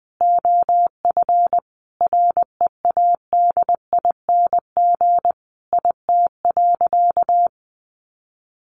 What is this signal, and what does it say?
Morse code, heard as a keyed tone, about 20 wpm, 700 Hz. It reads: OF READING IT.
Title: Sentences from Top 1200 Words 20wpm